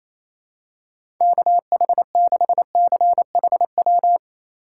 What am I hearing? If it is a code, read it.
Morse code, heard as a keyed tone, about 28 wpm, 700 Hz. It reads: KH6CHW
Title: USA Call Signs - 6 Characters - 2x Character Spacing - 28wpm